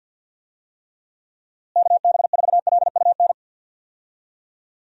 Morse code, heard as a keyed tone, about 50 wpm, 700 Hz. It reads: KB4LUN
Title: USA Call Signs - 6 Characters - Repeated 2x - 50wpm